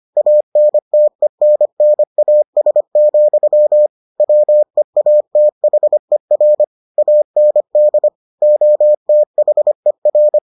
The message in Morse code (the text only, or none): ANTENNAS, WEATHER AND OTHER